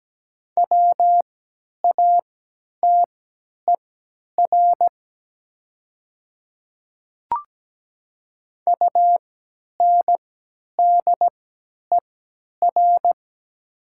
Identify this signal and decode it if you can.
Morse code, heard as a keyed tone, about 17 wpm, spaced as for 10 wpm, 700 Hz. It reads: WATER UNDER
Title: 3x Character Spacing - Top 300 Words - 17wpm